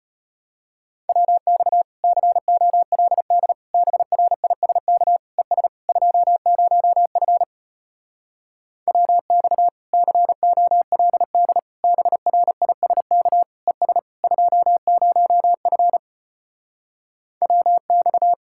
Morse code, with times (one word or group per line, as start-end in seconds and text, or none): 1.1-1.8 WX
2.0-3.5 COLD
3.7-5.2 BRISK
5.4-5.7 ES
5.9-7.4 20F
8.9-9.7 WX
9.9-11.6 COLD
11.8-13.4 BRISK
13.7-14.0 ES
14.2-16.0 20F
17.4-18.3 WX